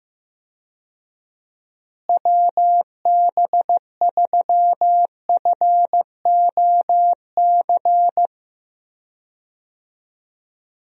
Morse code, heard as a keyed tone, about 15 wpm, 700 Hz. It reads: WB3FOC
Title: USA Call Signs - 6 Characters - Repeated 2x - 15wpm